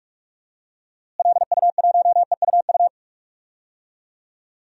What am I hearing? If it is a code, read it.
Morse code, heard as a keyed tone, about 46 wpm, 700 Hz. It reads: LU1EUU
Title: DXCC-100 Call Signs - 6 Characters - Repeated 2x - 45wpm